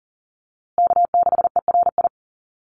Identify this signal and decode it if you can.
Morse code, heard as a keyed tone, about 37 wpm, 700 Hz. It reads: K6ERI
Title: USA Call Signs Repeated 3x 40wpm